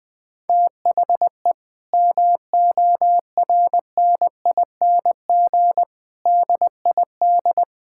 Morse code, heard as a keyed tone, about 20 wpm, 700 Hz. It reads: THE MORNING DID
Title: Sentences from Top 1600 Words 20wpm